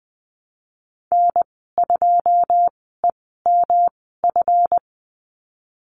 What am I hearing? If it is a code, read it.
Morse code, heard as a keyed tone, about 20 wpm, spaced as for 15 wpm, 700 Hz. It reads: N2EMF
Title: 173 Lesson 28g - Number 9 - Call Signs 20wpm